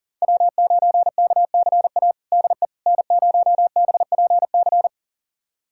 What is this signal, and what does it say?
Morse code, heard as a keyed tone, about 40 wpm, 700 Hz. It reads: W9KCA DE N0BPC
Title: USA Callsign DE Callsign - 5 Characters - 40wpm